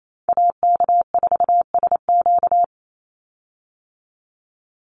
Morse code, heard as a keyed tone, about 28 wpm, 700 Hz. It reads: AK4SQ
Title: Call Signs - 28wpm